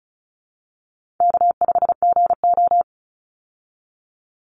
Morse code, heard as a keyed tone, about 34 wpm, 700 Hz. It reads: K5GO
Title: Call Signs Repeated 3x - 35wpm